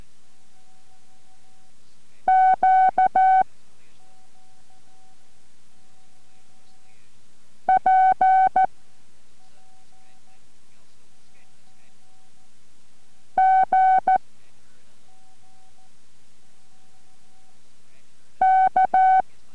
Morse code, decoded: Q P G K